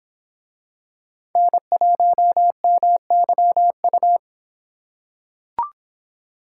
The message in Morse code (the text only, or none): N1MYU